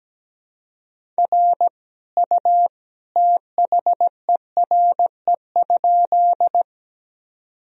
Morse code, heard as a keyed tone, about 17 wpm, 700 Hz. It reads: R U THERE?
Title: Sentences from Top 100 Words and Abbreviations 17wpm